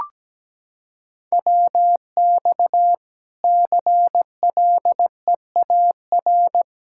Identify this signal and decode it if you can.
Morse code, heard as a keyed tone, about 17 wpm, 700 Hz. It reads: WX CLEAR
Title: Practice QSO - Exchange 2 - Info - 17wpm